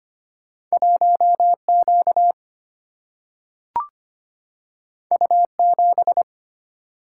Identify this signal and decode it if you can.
Morse code, heard as a keyed tone, about 25 wpm, 700 Hz. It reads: 1Q U7
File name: Two Letter-Number Combinations - 25wpm